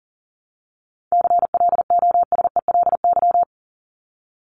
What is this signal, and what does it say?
Morse code, heard as a keyed tone, about 37 wpm, 700 Hz. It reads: CLOSELY
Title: LY Suffix - Top 100 Words with Base Words 40wpm